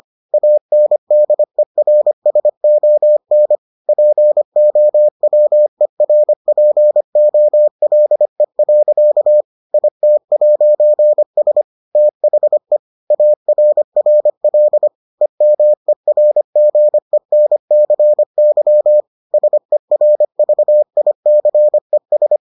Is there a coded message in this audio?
Yes: Morse code from 0.3 to 22.4 s, heard as a keyed tone, about 25 wpm, 600 Hz. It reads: ANDERSON POWERPOLE. IT'S THE ARRL EMERGENCY SERVICES